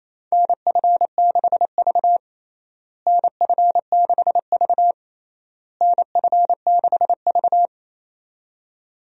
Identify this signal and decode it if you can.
Morse code, heard as a keyed tone, about 28 wpm, 700 Hz. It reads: NF6V NF6V NF6V
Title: USA Call Signs - 4 Characters - Repeated 3x - 28wpm